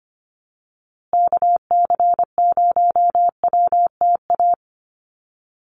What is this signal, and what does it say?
Morse code, heard as a keyed tone, about 25 wpm, 700 Hz. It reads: KC0WTA
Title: USA Call Signs Repeated 3x 25wpm